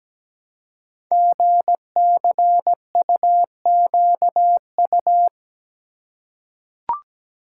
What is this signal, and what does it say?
Morse code, heard as a keyed tone, about 17 wpm, 700 Hz. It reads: GCUQU
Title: Code Groups - 5 Letters - 17wpm